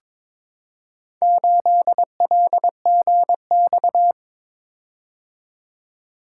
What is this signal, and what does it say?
Morse code, heard as a keyed tone, about 22 wpm, 700 Hz. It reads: 8LGX
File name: practice / 22wpm / Four Letter-Number Combinations - 22wpm